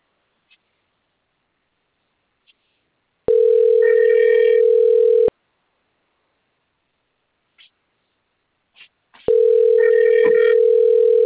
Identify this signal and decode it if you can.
Ringback tone (North America), 440 + 480 Hz.
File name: rg-1600-unknown- - - .27627